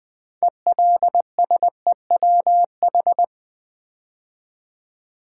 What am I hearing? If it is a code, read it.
Morse code, heard as a keyed tone, about 20 wpm, 700 Hz. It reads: ELSEWH
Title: Build A Word - 3 Letters At A Time - 9-Letter Words - 20wpm